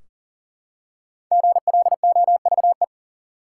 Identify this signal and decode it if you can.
Morse code, heard as a keyed tone, about 40 wpm, 700 Hz. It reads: GLOVE
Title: Five Letter Words - 40wpm